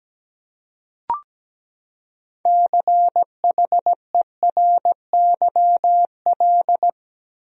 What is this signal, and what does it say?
Morse code, heard as a keyed tone, about 17 wpm, 700 Hz. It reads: CHERYL
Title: USA Common Names - 17wpm